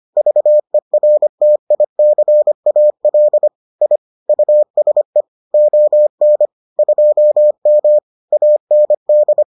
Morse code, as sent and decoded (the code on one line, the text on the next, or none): ...- . .-. - .. -.-. .- .-.. / .. / ..- ... / --- -. / ..--- -- / .- -. -..
VERTICAL I USE ON 2M AND